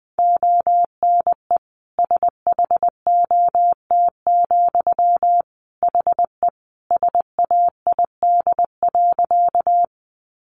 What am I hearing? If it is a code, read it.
Morse code, heard as a keyed tone, about 20 wpm, 700 Hz. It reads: ONE SHOT, HE SAID.
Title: Sentences from Top 1100 Words 20wpm